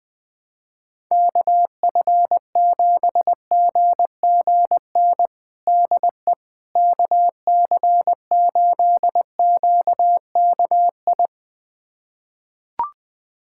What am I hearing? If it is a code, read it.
Morse code, heard as a keyed tone, about 20 wpm, 700 Hz. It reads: KF7GGN DE KC8QKI
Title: USA Callsign DE Callsign 20wpm